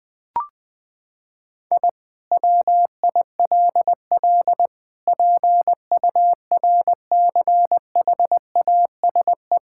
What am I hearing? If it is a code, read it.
Morse code, heard as a keyed tone, about 20 wpm, 700 Hz. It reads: I WILL PURCHASE
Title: Sentences from Top 1000 Words - Review Part 5 - 20wpm